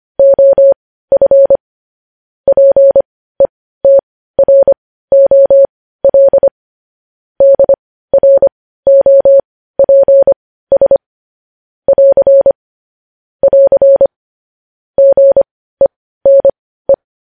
Morse code, heard as a keyed tone, about 25 wpm, spaced as for 15 wpm, 550 Hz. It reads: OF PETROL DROPS <AR> <AR> GENE
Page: News Headlines in Morse Code at 15 WPM